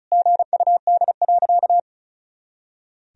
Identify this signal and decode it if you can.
Morse code, heard as a keyed tone, about 36 wpm, 700 Hz. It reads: GUD.
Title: Sentences from Top 500 Words and Abbreviations 35wpm